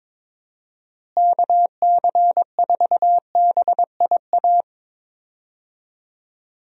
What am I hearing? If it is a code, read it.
Morse code, heard as a keyed tone, about 21 wpm, 700 Hz. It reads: KC4BIA
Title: USA Call Signs - 6 Characters - Repeated 2x - 22wpm